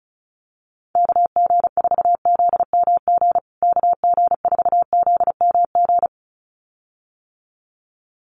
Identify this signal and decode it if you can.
Morse code, heard as a keyed tone, about 34 wpm, 700 Hz. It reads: KG4ZMG KG4ZMG
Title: USA Call Signs Repeated 2x 35wpm